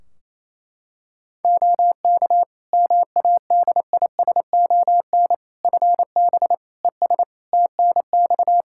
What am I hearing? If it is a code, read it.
Morse code, heard as a keyed tone, about 28 wpm, 700 Hz. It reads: OK MADISON FB ES TNX
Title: Practice QSO - Exchange 2 - Info - 28wpm